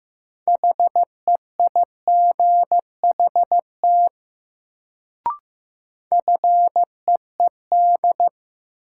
Morse code, heard as a keyed tone, about 15 wpm, 700 Hz. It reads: HEIGHT FEED
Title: Sentences from Top 1600 Words - New Word Drill Alt - 15wpm